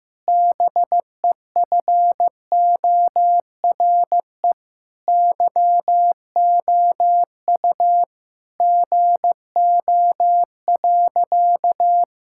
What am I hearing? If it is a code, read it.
Morse code, heard as a keyed tone, about 15 wpm, 700 Hz. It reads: BEFORE YOU GO.
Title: Sentences from Top 1000 Words - Review Part 6 - 15wpm